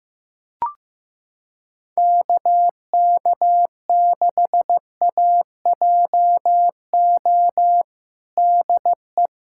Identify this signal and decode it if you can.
Morse code, heard as a keyed tone, about 15 wpm, 700 Hz. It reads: KK6AJO DE
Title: USA Callsign DE Callsign 15wpm